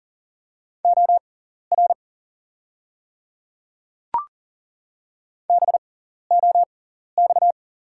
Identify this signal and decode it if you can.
Morse code, heard as a keyed tone, about 38 wpm, spaced as for 14 wpm, 690 Hz. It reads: OR BOX
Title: 6x Character Spacing - Top 500 Words - 40wpm